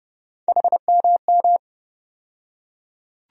Morse code, heard as a keyed tone, about 30 wpm, 700 Hz. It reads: HMM
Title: USA Call Sign Postfixes - 3 Letters 30wpm